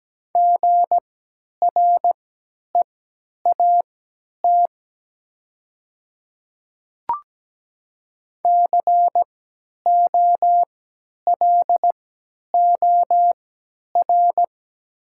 Morse code, heard as a keyed tone, about 17 wpm, spaced as for 10 wpm, 700 Hz. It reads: GREAT COLOR
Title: 3x Character Spacing - Top 500 Words - 17wpm